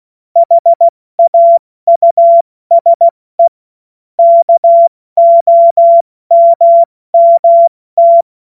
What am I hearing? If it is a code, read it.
Morse code, heard as a keyed tone, about 16 wpm, 680 Hz. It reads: HAUSE KOMMT
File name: Witz-16-September-2025-70-F80